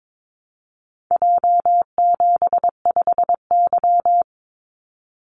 Morse code, heard as a keyed tone, about 22 wpm, 700 Hz. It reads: J75Y
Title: Call Signs Repeated 3x - 22wpm